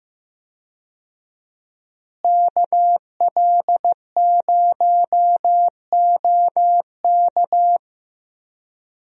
Morse code, heard as a keyed tone, about 15 wpm, 700 Hz. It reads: KL0OK
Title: USA Call Signs - 5 Characters - Repeated 2x - 15wpm